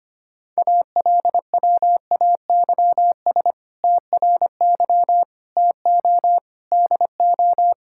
Morse code, heard as a keyed tone, about 25 wpm, 700 Hz. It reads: ALWAYS TRY TO DO
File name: practice / 25wpm / Sentences with 2-Word Phrases - 25wpm